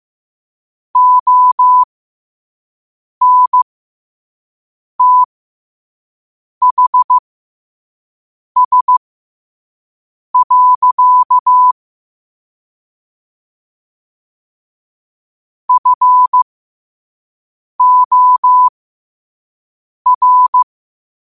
Morse code, heard as a keyed tone, about 15 wpm, spaced as for 5 wpm, 1000 Hz. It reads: ONTHS. FOR
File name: qotdfiles / 5wpm